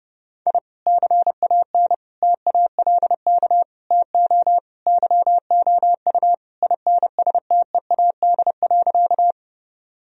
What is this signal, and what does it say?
Morse code, heard as a keyed tone, about 30 wpm, 700 Hz. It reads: I CAN TALK TO YOU INSTEAD.
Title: Sentences from Top 1000 Words - Review Part 4 - 30wpm